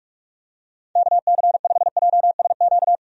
Morse code, heard as a keyed tone, about 46 wpm, 700 Hz. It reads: KC5JSQ
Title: USA Call Signs - 6 Characters - 45wpm